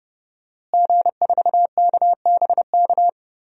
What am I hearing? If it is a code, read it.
Morse code, heard as a keyed tone, about 30 wpm, 700 Hz. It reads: G4KBK
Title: DXCC-100 Call Signs - 5 Characters - Repeated 2x - 30wpm